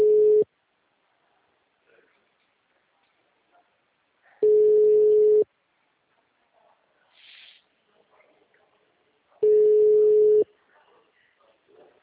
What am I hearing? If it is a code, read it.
Ringback tone (Europe, 425 Hz), with speech.
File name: out- -2016- - - .96130